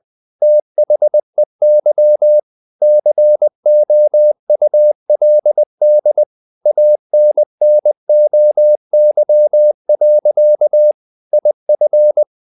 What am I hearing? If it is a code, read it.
Morse code, heard as a keyed tone, about 20 wpm, 600 Hz. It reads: THEY COULD ANNOY. IF
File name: the old man and the sea / the old man and the sea part 1 20wpm